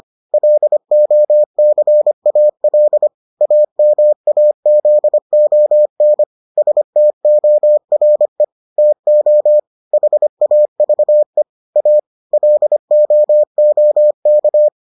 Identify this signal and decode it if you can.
Morse code, heard as a keyed tone, about 25 wpm, 600 Hz. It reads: LOCAL AMAZON STORE TO HAVE A LOOK